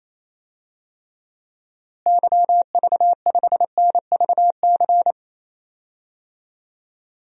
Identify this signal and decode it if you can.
Morse code, heard as a keyed tone, about 27 wpm, 700 Hz. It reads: YV5NVC
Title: DXCC-100 Call Signs - 6 Characters - Repeated 3x - 28wpm